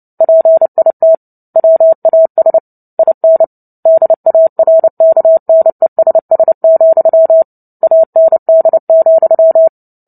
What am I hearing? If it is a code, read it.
Morse code, heard as a keyed tone, about 30 wpm, 650 Hz. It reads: PIT WAS IN DARKNESS, AND,
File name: Book Two / 03-Chapter 3 - 29 WPM